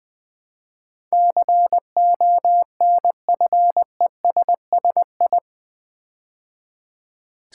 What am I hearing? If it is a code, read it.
Morse code, heard as a keyed tone, about 20 wpm, 700 Hz. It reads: CONFESSI
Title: Build A Word - 4 Letters At A Time - 10-Letter Words - 20wpm